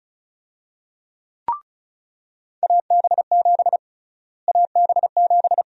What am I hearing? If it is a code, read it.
Morse code, heard as a keyed tone, about 36 wpm, 700 Hz. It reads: AB7 AB7
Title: USA Call Sign Prefixes - Repeated 2x - 35wpm